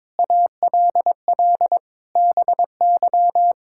Morse code, heard as a keyed tone, about 22 wpm, 700 Hz. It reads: ALL BY
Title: Sentences from Top 1000 Words - Review Part 5 - 22wpm